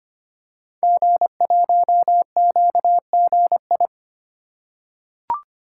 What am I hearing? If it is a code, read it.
Morse code, heard as a keyed tone, about 25 wpm, 700 Hz. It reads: G1QGI
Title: Code Groups - 5 Letter:Numbers - 25wpm